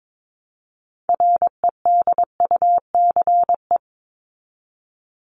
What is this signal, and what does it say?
Morse code, heard as a keyed tone, about 22 wpm, 700 Hz. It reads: REDUCE
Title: Top 1001-1250 Words 22wpm